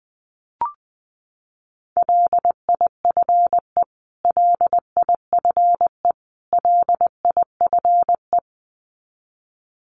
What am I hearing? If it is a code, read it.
Morse code, heard as a keyed tone, about 20 wpm, 700 Hz. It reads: LIFE LIFE LIFE
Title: Top 300 Words Repeated 20wpm